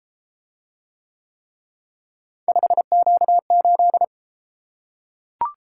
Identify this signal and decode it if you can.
Morse code, heard as a keyed tone, about 31 wpm, 700 Hz. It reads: 5Q8